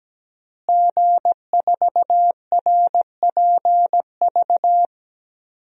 Morse code, heard as a keyed tone, about 17 wpm, 700 Hz. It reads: G4RPV
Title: DXCC-100 Call Signs - 5 Characters - Repeated 3x - 17wpm